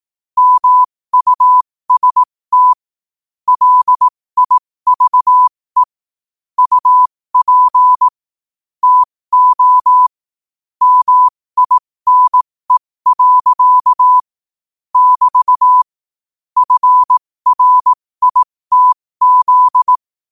Morse code, heard as a keyed tone, about 18 wpm, spaced as for 15 wpm, 1000 Hz. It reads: MUST LIVE UP TO MINE. <BT> FRITZ
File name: qotdfiles / 15wpm